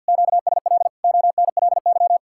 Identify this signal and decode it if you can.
Morse code, heard as a keyed tone, about 66 wpm, spaced as for 58 wpm, 700 Hz. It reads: QSL ONLY
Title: Speed-Racing Common QSO Phrases - 50wpm